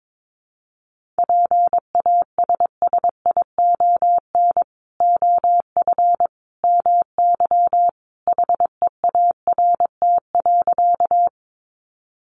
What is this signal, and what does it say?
Morse code, heard as a keyed tone, about 22 wpm, 700 Hz. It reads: PASSION OF MY HEART.
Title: Sentences from Top 1300 Words 22wpm